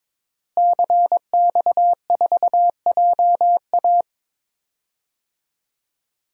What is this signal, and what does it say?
Morse code, heard as a keyed tone, about 22 wpm, 700 Hz. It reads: CX4JA
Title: DXCC-100 Call Signs - 5 Characters - Repeated 2x - 22wpm